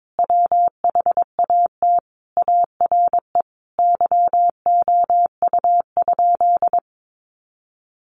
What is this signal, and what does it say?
Morse code, heard as a keyed tone, about 22 wpm, 700 Hz. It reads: WHAT ARE YOU?
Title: Sentences from Top 100 Words 22wpm